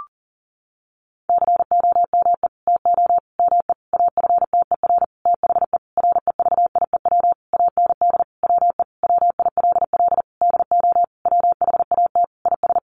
Morse code, heard as a keyed tone, about 40 wpm, 700 Hz. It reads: COME TO ME AFTER THE REVIEW AND WE WILL DO WHAT IS
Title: Sentences from Top 900 Words 40wpm